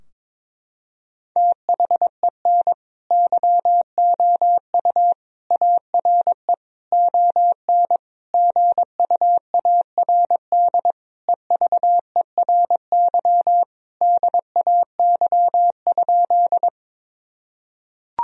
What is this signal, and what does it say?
Morse code, heard as a keyed tone, about 22 wpm, 700 Hz. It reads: THEN YOU ARE ON GUARD EVERY DAY?
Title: Sentences from Top 1000 Words - Review Part 4 - 22wpm